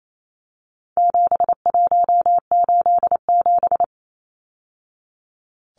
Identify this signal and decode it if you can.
Morse code, heard as a keyed tone, about 28 wpm, 700 Hz. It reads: 7187
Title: Numbers - 4-Digit 28wpm